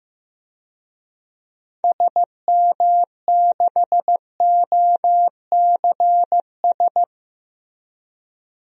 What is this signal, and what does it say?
Morse code, heard as a keyed tone, about 16 wpm, 700 Hz. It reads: SM6OCS
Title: DXCC-100 Call Signs - 6 Characters - 15wpm